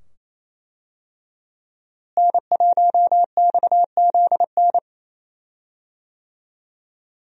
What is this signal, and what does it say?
Morse code, heard as a keyed tone, about 27 wpm, 700 Hz. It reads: N1XZN
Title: USA Call Signs - 5 Characters - Repeated 2x - 28wpm